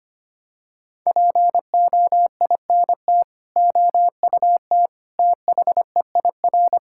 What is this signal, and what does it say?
Morse code, heard as a keyed tone, about 25 wpm, 700 Hz. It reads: POINT OUT THEIR
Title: Sentences from Top 1000 Words - Review Part 6 - 25wpm